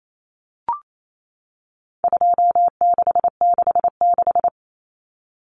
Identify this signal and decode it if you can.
Morse code, heard as a keyed tone, about 28 wpm, 700 Hz. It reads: 2666
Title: Common Cut Numbers - 4-Digit 28wpm